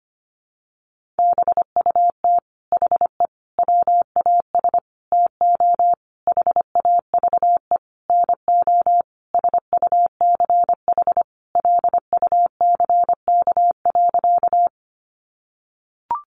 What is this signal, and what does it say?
Morse code, heard as a keyed tone, about 25 wpm, 700 Hz. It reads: BUT HE WAS TO HAVE NO SUCH LUCK.
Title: Sentences from Top 1500 Words 25wpm